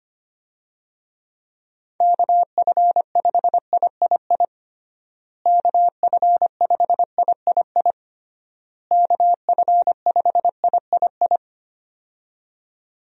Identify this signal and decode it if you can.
Morse code, heard as a keyed tone, about 25 wpm, 700 Hz. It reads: KF5III KF5III KF5III
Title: USA Call Signs - 6 Characters - Repeated 3x - 25wpm